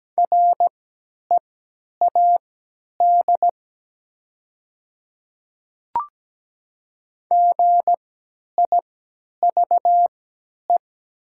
Morse code, heard as a keyed tone, about 17 wpm, spaced as for 10 wpm, 700 Hz. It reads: READ GIVE
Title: 3x Character Spacing - Top 300 Words - 17wpm